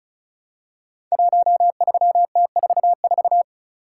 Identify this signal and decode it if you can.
Morse code, heard as a keyed tone, about 36 wpm, 690 Hz. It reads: 13T44
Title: Code Groups - 5 Letter:Numbers - 35wpm